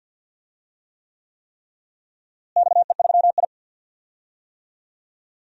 Morse code, heard as a keyed tone, about 49 wpm, 700 Hz. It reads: XE3I
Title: DXCC-100 Call Signs - 4 Characters - Repeated 3x - 50wpm